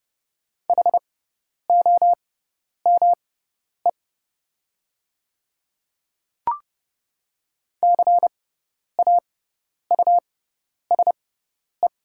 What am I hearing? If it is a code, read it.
Morse code, heard as a keyed tone, about 30 wpm, spaced as for 10 wpm, 700 Hz. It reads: HOME CAUSE
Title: 6x Character Spacing - Top 300 Words - 30wpm